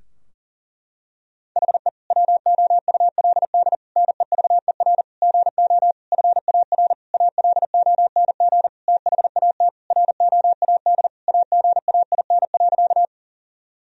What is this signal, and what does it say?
Morse code, heard as a keyed tone, about 40 wpm, 700 Hz. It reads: HE WOULD NEVER GO FAR ALONG THAT ROAD AGAIN.
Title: Sentences from Top 1000 Words - Review Part 2 - 40wpm